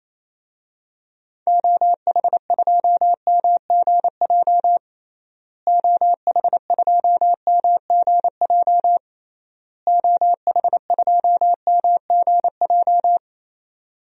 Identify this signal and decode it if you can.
Morse code, heard as a keyed tone, about 28 wpm, 700 Hz. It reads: OH2MGJ OH2MGJ OH2MGJ
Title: DXCC-100 Call Signs - 6 Characters - Repeated 3x - 28wpm